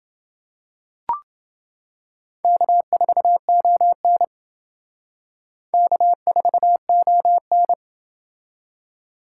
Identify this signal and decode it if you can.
Morse code, heard as a keyed tone, about 29 wpm, 700 Hz. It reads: K4ON K4ON